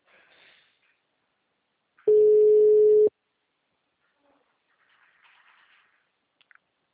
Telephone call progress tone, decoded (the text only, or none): ringback tone